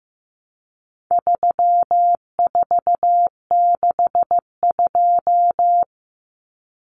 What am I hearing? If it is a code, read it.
Morse code, heard as a keyed tone, about 15 wpm, 700 Hz. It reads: 3462
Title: Common Cut Numbers - 4-Digit 15wpm